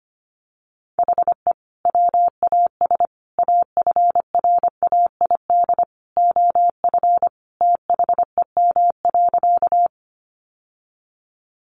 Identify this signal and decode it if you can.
Morse code, heard as a keyed tone, about 25 wpm, 700 Hz. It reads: HE WAS AFRAID OF THEM.
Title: Sentences from Top 1500 Words 25wpm